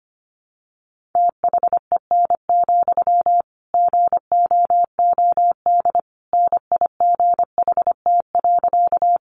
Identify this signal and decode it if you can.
Morse code, heard as a keyed tone, about 25 wpm, 700 Hz. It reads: THEN, GOOD NIGHT.
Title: Sentences from Top 300 Words 25wpm